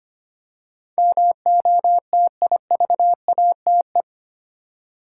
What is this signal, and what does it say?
Morse code, heard as a keyed tone, about 25 wpm, 700 Hz. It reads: MOTIVATE
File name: 8 Letter Words - First 3 Letters Given - 25wpm